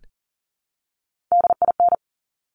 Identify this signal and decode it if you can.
Morse code, heard as a keyed tone, about 40 wpm, 700 Hz. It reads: DIN
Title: Top Three Letter Combinations in Words 40wpm